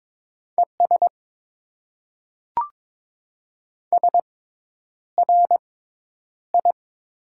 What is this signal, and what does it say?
Morse code, heard as a keyed tone, about 22 wpm, spaced as for 10 wpm, 700 Hz. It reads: ES SRI
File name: Top 90 QSO Elements - 6x Character Spacing - 22wpm